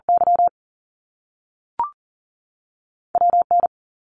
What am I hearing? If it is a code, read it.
Morse code, heard as a keyed tone, about 40 wpm, 700 Hz. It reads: Y WN
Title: Top Two Letter Combinations in Words 40wpm